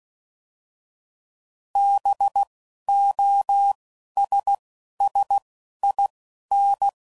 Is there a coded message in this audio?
Yes: Morse code from 1.8 to 6.9 s, heard as a keyed tone, about 16 wpm, 800 Hz. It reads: B O S S I N